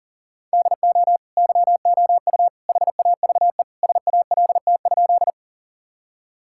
Morse code, heard as a keyed tone, about 37 wpm, 700 Hz. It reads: DO YOU HAVE SALT?
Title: Sentences from Top 1000 Words - Review Part 5 - 40wpm